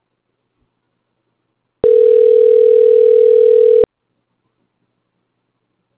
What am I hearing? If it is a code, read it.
Ringback tone (North America), 440 + 480 Hz.